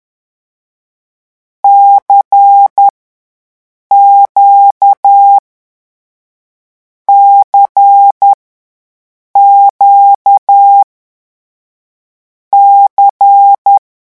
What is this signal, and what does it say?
Morse code, heard as a keyed tone, about 10 wpm, 800 Hz. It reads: C Q C Q C
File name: morse mp3 / lecon2-vitesse 10